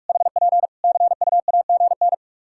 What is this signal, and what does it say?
Morse code, heard as a keyed tone, about 46 wpm, 700 Hz. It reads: HP CUAGN
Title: Practice QSO - Exchange 4 - Ending - 45wpm